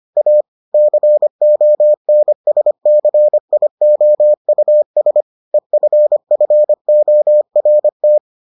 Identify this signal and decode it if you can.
Morse code, heard as a keyed tone, about 25 wpm, 600 Hz. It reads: A CONSCIOUS EFFORT